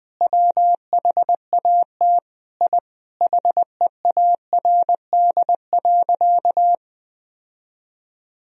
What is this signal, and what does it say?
Morse code, heard as a keyed tone, about 20 wpm, 700 Hz. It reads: WHAT I HEARD.
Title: Sentences from Top 1000 Words - Review Part 4 - 20wpm